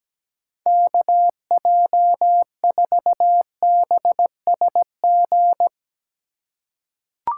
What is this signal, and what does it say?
Morse code, heard as a keyed tone, about 17 wpm, 700 Hz. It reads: KJ4BSG
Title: USA Call Signs - 6 Characters - 3x Character Spacing - 17wpm